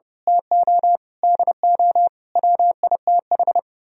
Morse code, heard as a keyed tone, about 28 wpm, 700 Hz. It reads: TO DO WITH
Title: Speed-Racing - Sets of 5 Words 30wpm